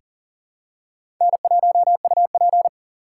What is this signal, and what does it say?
Morse code, heard as a keyed tone, about 40 wpm, 700 Hz. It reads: N1UP
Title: USA Call Signs - 4 Characters - 4x Character Spacing - 40wpm